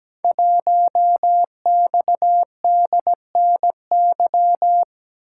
Morse code, heard as a keyed tone, about 17 wpm, 700 Hz. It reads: 1XDNY
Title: Code Groups - 5 Letter:Numbers - 17wpm